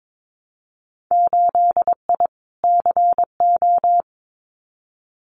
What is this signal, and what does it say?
Morse code, heard as a keyed tone, about 22 wpm, 700 Hz. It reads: 8I CO
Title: Winter Field Day Exchanges Encore 22wpm